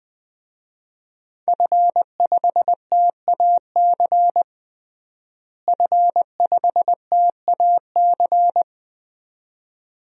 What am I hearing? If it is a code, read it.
Morse code, heard as a keyed tone, about 21 wpm, 700 Hz. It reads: F5TAC F5TAC
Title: DXCC-100 Call Signs - 5 Characters - Repeated 2x - 20wpm